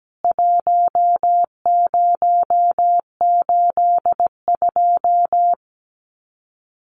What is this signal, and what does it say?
Morse code, heard as a keyed tone, about 17 wpm, 700 Hz. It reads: 1082
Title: CWT Exchange 17wpm